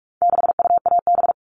Morse code, heard as a keyed tone, about 44 wpm, 700 Hz. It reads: 6UAB
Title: All Cut Numbers - 4-Digit 45wpm